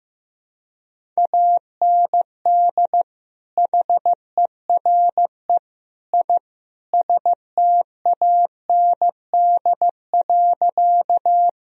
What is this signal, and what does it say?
Morse code, heard as a keyed tone, about 15 wpm, 700 Hz. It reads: AND HERE I STAND.
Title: Sentences from Top 1000 Words - Review Part 1 - 15wpm